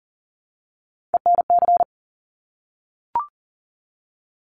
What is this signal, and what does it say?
Morse code, heard as a keyed tone, about 41 wpm, 700 Hz. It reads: ENC
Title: Top Three Letter Combinations in Words 40wpm